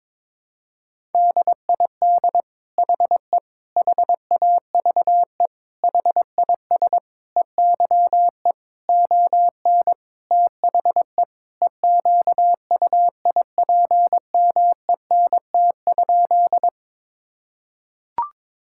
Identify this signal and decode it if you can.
Morse code, heard as a keyed tone, about 22 wpm, 700 Hz. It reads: DID HE HAVE HIS EYE ON THE EQUIPMENT?
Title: Sentences from Top 1000 Words - Review Part 5 - 22wpm